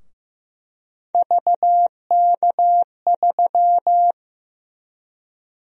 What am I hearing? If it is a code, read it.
Morse code, heard as a keyed tone, about 15 wpm, 700 Hz. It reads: VK3
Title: USA Call Sign Prefixes - 15wpm